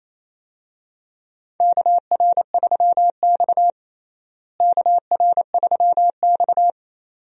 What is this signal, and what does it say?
Morse code, heard as a keyed tone, about 28 wpm, 700 Hz. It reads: KR3X KR3X
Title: USA Call Signs - 4 Characters - Repeated 2x - 28wpm